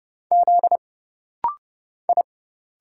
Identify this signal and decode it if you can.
Morse code, heard as a keyed tone, about 30 wpm, 700 Hz. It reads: Z I
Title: Single Letter-Number Alt - Rapid-Fire 30wpm